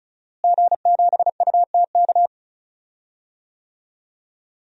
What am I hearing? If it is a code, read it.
Morse code, heard as a keyed tone, about 34 wpm, 700 Hz. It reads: G7UTK
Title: DXCC-100 Call Signs - 5 Characters - Repeated 2x - 35wpm